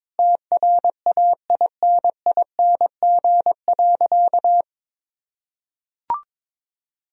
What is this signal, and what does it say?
Morse code, heard as a keyed tone, about 22 wpm, 700 Hz. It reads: TRAINING.
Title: Sentences from Top 1000 Words - Review Part 4 - 22wpm